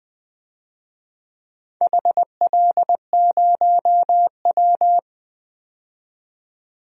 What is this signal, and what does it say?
Morse code, heard as a keyed tone, about 20 wpm, 700 Hz. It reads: HL0W
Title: DXCC-100 Call Signs - 4 Characters - 20wpm